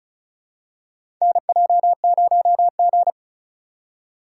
Morse code, heard as a keyed tone, about 34 wpm, 700 Hz. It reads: NJ0G
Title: USA Call Signs - 4 Characters - 2x Character Spacing - 35wpm